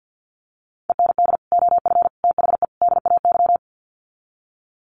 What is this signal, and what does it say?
Morse code, heard as a keyed tone, about 50 wpm, 700 Hz. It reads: END OF THE DAY
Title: Sets of 4 Words 50wpm